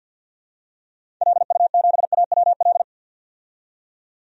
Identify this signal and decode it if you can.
Morse code, heard as a keyed tone, about 52 wpm, 700 Hz. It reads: LU7AWL
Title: DXCC-100 Call Signs - 6 Characters - 50wpm